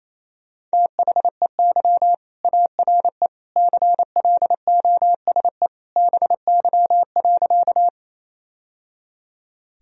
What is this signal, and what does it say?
Morse code, heard as a keyed tone, about 28 wpm, 700 Hz. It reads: THEY ARE CLOSE BY.
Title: Sentences from Top 1000 Words - Review Part 1 - 28wpm